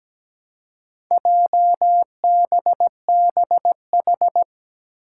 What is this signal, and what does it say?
Morse code, heard as a keyed tone, about 17 wpm, 700 Hz. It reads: JBBH
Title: Four Challenging Letter-Number Combinations 17wpm